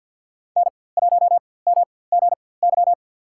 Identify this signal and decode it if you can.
Morse code, heard as a keyed tone, about 52 wpm, spaced as for 23 wpm, 700 Hz. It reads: N1MGY
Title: USA Call Signs - 5 Characters - 4x Character Spacing - 50wpm